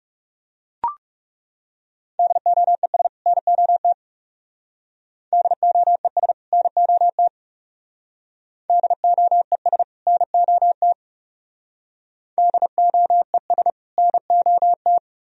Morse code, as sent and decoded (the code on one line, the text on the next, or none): -.. --- . ... / -. --- - / -.. --- . ... / -. --- - / -.. --- . ... / -. --- - / -.. --- . ... / -. --- -
DOES NOT DOES NOT DOES NOT DOES NOT